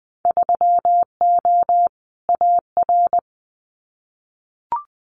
Morse code, heard as a keyed tone, about 20 wpm, 700 Hz. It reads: 3O AR
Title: Winter Field Day Exchanges Encore 20wpm